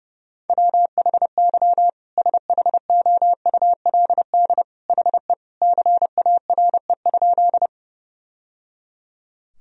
Morse code, heard as a keyed tone, about 32 wpm, 700 Hz. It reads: WHY SHOULD HE CARE?
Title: Sentences from Top 1000 Words - Review Part 1 - 30wpm